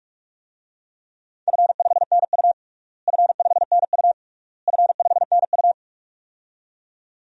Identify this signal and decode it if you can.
Morse code, heard as a keyed tone, about 46 wpm, 700 Hz. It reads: F5NU F5NU F5NU
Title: DXCC-100 Call Signs - 4 Characters - Repeated 3x - 45wpm